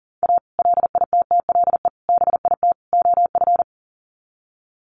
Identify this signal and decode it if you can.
Morse code, heard as a keyed tone, about 40 wpm, 700 Hz. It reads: A LITTLE BIT OF
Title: Sets of 4 Words 40wpm